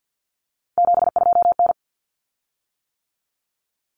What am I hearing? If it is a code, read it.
Morse code, heard as a keyed tone, about 48 wpm, 700 Hz. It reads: 72N
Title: Common Cut Numbers - 3-Digit 50wpm